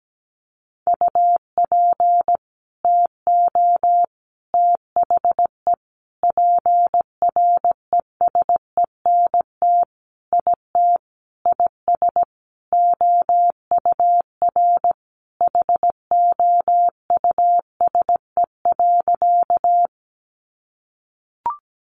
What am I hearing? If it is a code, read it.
Morse code, heard as a keyed tone, about 17 wpm, 700 Hz. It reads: UP TO THE PRESENT IT IS OUR HOUSE.
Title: Sentences from Top 800 Words 17wpm